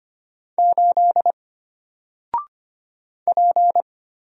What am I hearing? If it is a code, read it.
Morse code, heard as a keyed tone, about 25 wpm, 700 Hz. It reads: 8 P
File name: Single-Letter-Number-25wpm